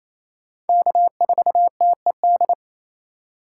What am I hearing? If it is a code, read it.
Morse code, heard as a keyed tone, about 34 wpm, 700 Hz. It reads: K4TED
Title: Speed Racing - USA 1x3 Call Signs 28wpm